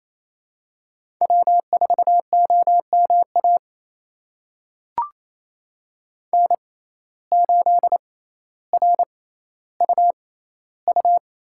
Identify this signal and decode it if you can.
Morse code, heard as a keyed tone, about 28 wpm, spaced as for 14 wpm, 700 Hz. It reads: W4OMA N8RUU
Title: USA Call Signs - 5 Characters - 6x Character Spacing - 28wpm